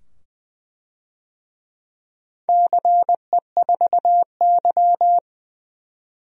Morse code, heard as a keyed tone, about 20 wpm, 700 Hz. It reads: CE4Y
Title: DXCC-100 Call Signs - 4 Characters - 20wpm